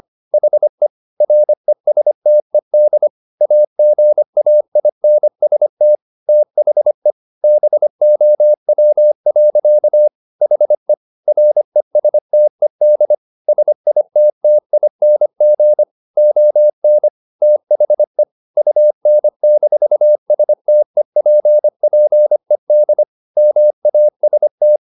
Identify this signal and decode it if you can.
Morse code, heard as a keyed tone, about 25 wpm, 600 Hz. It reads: HE RESTED AGAINST THE BOW. HE RESTED SITTING ON THE UN-STEPPED MAST
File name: the old man and the sea part 1 25wpm